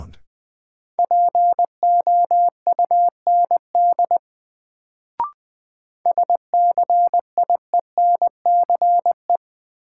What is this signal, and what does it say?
Morse code, heard as a keyed tone, about 20 wpm, 700 Hz. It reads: POUND SCIENCE
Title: Sentences from Top 1000 Words - Review Part 2 - New Word Drill - 20wpm